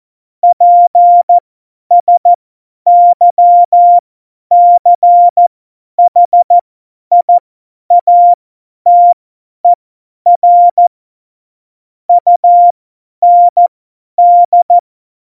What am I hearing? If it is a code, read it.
Morse code, heard as a keyed tone, about 14 wpm, spaced as for 10 wpm, 700 Hz. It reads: PSYCHIATER UND
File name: Morsekurs / Witze / Witz 09. Januar 2018, 50 F70